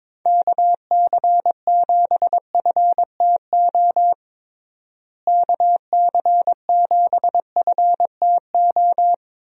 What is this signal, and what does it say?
Morse code, heard as a keyed tone, about 22 wpm, 700 Hz. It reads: KC7FTO KC7FTO
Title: USA Call Signs - 6 Characters - Repeated 2x - 22wpm